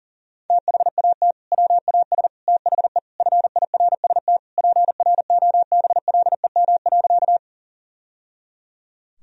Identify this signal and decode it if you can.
Morse code, heard as a keyed tone, about 37 wpm, 700 Hz. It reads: THAT WAS THE FIRST PROBLEM.
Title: Sentences from Top 1000 Words - Review Part 2 - 40wpm